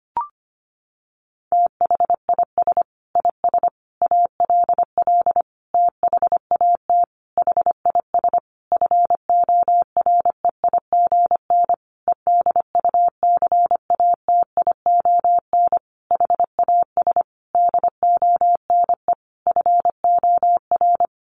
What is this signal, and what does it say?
Morse code, heard as a keyed tone, about 25 wpm, 700 Hz. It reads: THIS IS ALL THAT HIS FOREIGN EDUCATION HAS DONE FOR
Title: Sentences from Top 1400 Words 25wpm